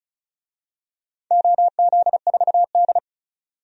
Morse code, heard as a keyed tone, about 36 wpm, 700 Hz. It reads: OZ4D
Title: DXCC-100 Call Signs - 4 Characters - 35wpm